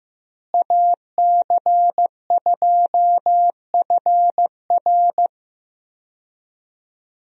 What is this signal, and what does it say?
Morse code, heard as a keyed tone, about 15 wpm, 700 Hz. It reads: AC2FR
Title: USA Call Signs - 5 Characters - Repeated 3x - 15wpm